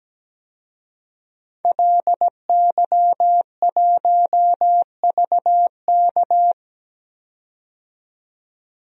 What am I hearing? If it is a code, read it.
Morse code, heard as a keyed tone, about 17 wpm, 700 Hz. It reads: LY1VK
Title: DXCC-100 Call Signs - 5 Characters - 17wpm